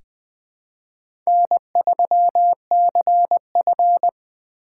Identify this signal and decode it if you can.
Morse code, heard as a keyed tone, about 20 wpm, 700 Hz. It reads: N3CF
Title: USA Call Signs - 4 Characters - 4x Character Spacing - 20wpm